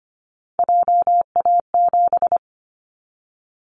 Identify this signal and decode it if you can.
Morse code, heard as a keyed tone, about 25 wpm, 700 Hz. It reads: JA7
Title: Callsign Prefixes for Top 50 DXCC - 25wpm